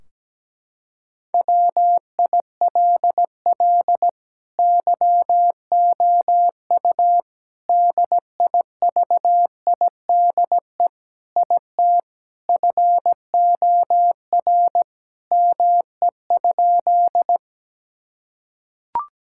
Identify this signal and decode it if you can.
Morse code, heard as a keyed tone, about 17 wpm, 700 Hz. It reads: WILL YOU DIVIDE IT FOR ME?
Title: Sentences from Top 1600 Words 17wpm